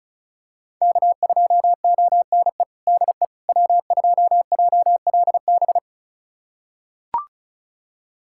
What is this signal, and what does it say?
Morse code, heard as a keyed tone, about 36 wpm, 700 Hz. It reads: K2ONE DE W2JLB
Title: USA Callsign DE Callsign - 5 Characters - 35wpm